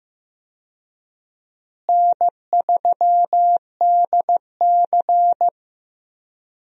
Morse code, heard as a keyed tone, about 15 wpm, 700 Hz. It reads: N3DC
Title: USA Call Signs - 4 Characters - Repeated 2x - 15wpm